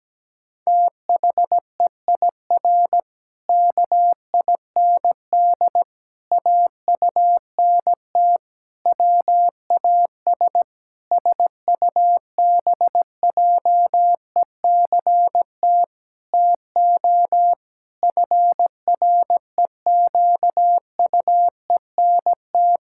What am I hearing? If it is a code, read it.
Morse code, heard as a keyed tone, about 17 wpm, 700 Hz. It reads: THEIR KIND AUNT WAS SUBJECT TO FREQUENT
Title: Pangram Challenge v2 - 17wpm